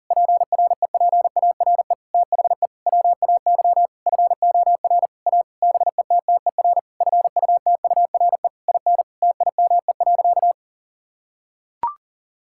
Morse code, heard as a keyed tone, about 40 wpm, 700 Hz. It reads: PREPARE THE WAY FOR A BETTER FUTURE IN TIME.
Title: Sentences from Top 1000 Words - Review Part 6 - 40wpm